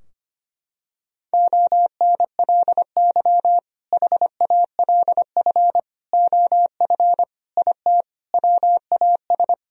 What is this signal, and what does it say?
Morse code, heard as a keyed tone, about 25 wpm, 700 Hz. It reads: ONLY HALF OF IT WAS
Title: Sentences from Top 1000 Words - Review Part 2 - 25wpm